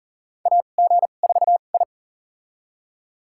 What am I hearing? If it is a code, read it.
Morse code, heard as a keyed tone, about 40 wpm, spaced as for 29 wpm, 700 Hz. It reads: AG4I
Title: USA Call Signs - 4 Characters - 2x Character Spacing - 40wpm